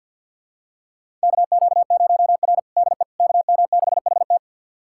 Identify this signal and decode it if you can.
Morse code, heard as a keyed tone, about 51 wpm, 700 Hz. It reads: KQ0R DE KM6HT
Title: USA Callsign DE Callsign 50wpm